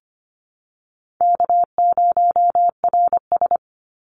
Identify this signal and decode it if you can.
Morse code, heard as a keyed tone, about 25 wpm, 700 Hz. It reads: K0RS
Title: NA SOTA Chaser Callsigns 25wpm